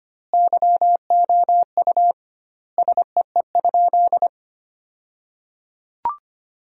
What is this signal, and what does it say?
Morse code, heard as a keyed tone, about 25 wpm, 700 Hz. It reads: YOU SEE?
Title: 2x Word-Spacing - Sentences from Top 100 Words - 25wpm